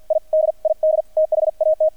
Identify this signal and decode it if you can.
Morse code, heard as a keyed tone, about 50 wpm, spaced as for 44 wpm, 630 Hz, with radio noise. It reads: I DID THAT
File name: archived / CRSnet